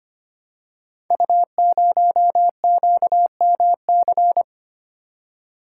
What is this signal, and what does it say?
Morse code, heard as a keyed tone, about 25 wpm, 700 Hz. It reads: U0QMC
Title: Code Groups - 5 Letter:Numbers - 25wpm